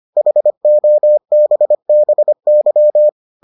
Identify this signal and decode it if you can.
Morse code, heard as a keyed tone, about 25 wpm, 600 Hz. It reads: HOBBY